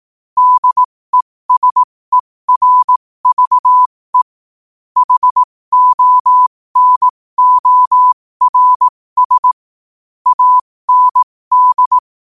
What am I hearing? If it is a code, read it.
Morse code, heard as a keyed tone, about 18 wpm, spaced as for 15 wpm, 1000 Hz. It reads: DESERVE HONORS AND
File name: qotdfiles / 15wpm